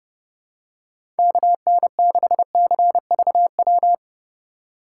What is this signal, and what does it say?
Morse code, heard as a keyed tone, about 32 wpm, 700 Hz. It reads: KN6CVW
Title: USA Call Signs - 6 Characters - 3x Character Spacing - 30wpm